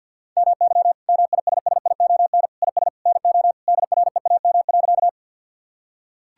Morse code, heard as a keyed tone, about 50 wpm, 700 Hz. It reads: MY MISSION IS NO DREAM.
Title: Sentences from Top 1600 Words 50wpm